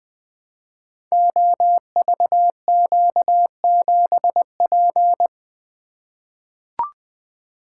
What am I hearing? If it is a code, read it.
Morse code, heard as a keyed tone, about 20 wpm, 700 Hz. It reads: OVQ7P
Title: Code Groups - 5 Letter:Numbers - 20wpm